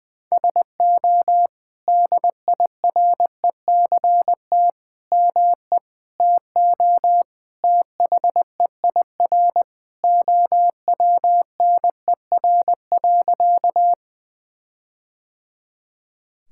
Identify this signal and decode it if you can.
Morse code, heard as a keyed tone, about 20 wpm, 700 Hz. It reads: SO DIRECT ME TO THEIR OWNER.
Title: Sentences from Top 1600 Words 20wpm